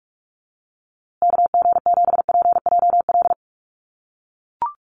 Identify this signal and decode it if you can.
Morse code, heard as a keyed tone, about 44 wpm, 700 Hz. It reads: KG7PJL
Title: USA Call Signs Repeated 3x 45wpm